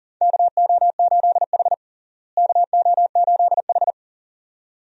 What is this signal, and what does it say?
Morse code, heard as a keyed tone, about 40 wpm, 700 Hz. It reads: KO8H KO8H
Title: USA Call Signs - 4 Characters - Repeated 2x - 40wpm